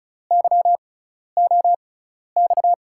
Morse code, heard as a keyed tone, about 36 wpm, spaced as for 12 wpm, 700 Hz. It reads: YOX
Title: USA Call Signs - 6x Character Spacing - 35wpm